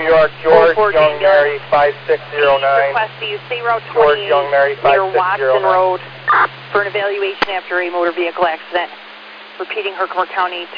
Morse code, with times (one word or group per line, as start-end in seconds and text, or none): none